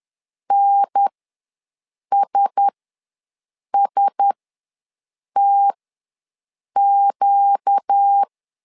Morse code, heard as a keyed tone, about 10 wpm, spaced as for 6 wpm, 800 Hz. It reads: NSSTQ
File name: LEZIONI / 4 Q1S ws 10wpm 2208